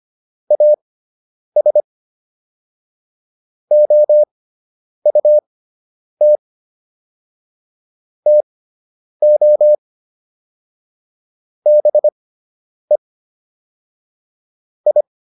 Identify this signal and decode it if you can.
Morse code, heard as a keyed tone, about 25 wpm, spaced as for 9 wpm, 600 Hz. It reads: AS OUT TO BE I